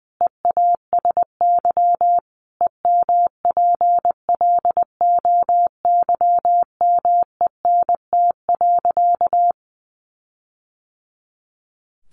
Morse code, heard as a keyed tone, about 20 wpm, 700 Hz. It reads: EASY EMPLOYMENT.
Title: Sentences from Top 1500 Words 20wpm